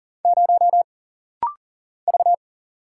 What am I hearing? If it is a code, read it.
Morse code, heard as a keyed tone, about 40 wpm, 700 Hz. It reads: 0 V
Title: Single Letter-Number Alt - Rapid-Fire 40wpm